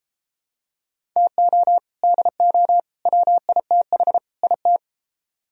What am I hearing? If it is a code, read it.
Morse code, heard as a keyed tone, about 31 wpm, 700 Hz. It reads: TO DO WITH IT